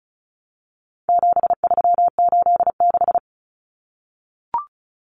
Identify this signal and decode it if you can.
Morse code, heard as a keyed tone, about 34 wpm, 700 Hz. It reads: 7386
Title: Common Cut Numbers - 4-Digit 35wpm